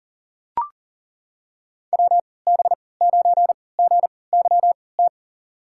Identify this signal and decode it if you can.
Morse code, heard as a keyed tone, about 40 wpm, spaced as for 23 wpm, 700 Hz. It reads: WB9GYT
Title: USA Call Signs - 6 Characters - 3x Character Spacing - 40wpm